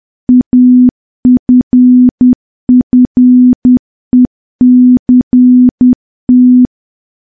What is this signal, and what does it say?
Morse code, heard as a keyed tone, about 10 wpm, 250 Hz. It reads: AFFECT
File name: morse / morse-training-10WPM